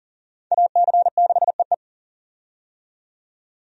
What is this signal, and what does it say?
Morse code, heard as a keyed tone, about 40 wpm, 700 Hz. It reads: AC6EE
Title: USA Call Signs - 5 Characters - 40wpm